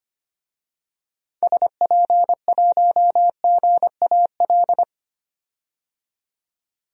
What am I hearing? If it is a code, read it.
Morse code, heard as a keyed tone, about 25 wpm, 700 Hz. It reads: SP1GAL
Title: DXCC-100 Call Signs - 6 Characters - 25wpm